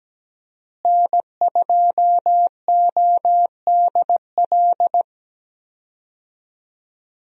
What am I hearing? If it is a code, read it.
Morse code, heard as a keyed tone, about 17 wpm, 700 Hz. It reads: N2ODL
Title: USA Call Signs - 5 Characters - Repeated 2x - 17wpm